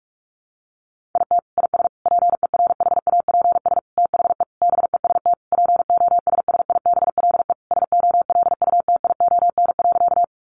Morse code, heard as a keyed tone, about 44 wpm, 700 Hz. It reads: IT IS PERHAPS THE BEST POSSIBLE SOLUTION.
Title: Sentences from Top 1300 Words 45wpm